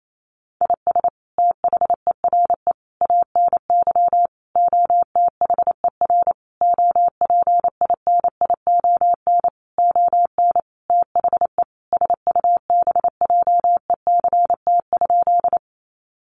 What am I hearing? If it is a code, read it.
Morse code, heard as a keyed tone, about 28 wpm, 700 Hz. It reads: IS THERE ANY OTHER OPINION ON THE SUBJECT?
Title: Sentences from Top 1200 Words 28wpm